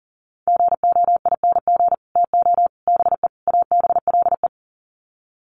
Morse code, heard as a keyed tone, about 40 wpm, 700 Hz. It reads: GOING TO BE ABLE
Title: Sets of 4 Words 40wpm